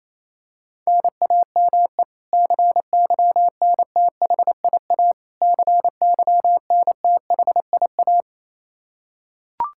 Morse code, heard as a keyed tone, about 28 wpm, 700 Hz. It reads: NAME CYNTHIA CYNTHIA
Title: K1USN SST QSO Practice 28wpm